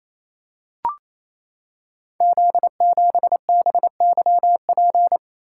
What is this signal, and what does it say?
Morse code, heard as a keyed tone, about 27 wpm, 700 Hz. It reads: Z7BYP
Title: Code Groups - 5 Letter:Numbers - 28wpm